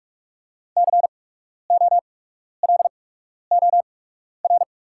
Morse code, heard as a keyed tone, about 44 wpm, spaced as for 12 wpm, 690 Hz. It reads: COLOR
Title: 8x Character Spacing - Top 500 Words - 45wpm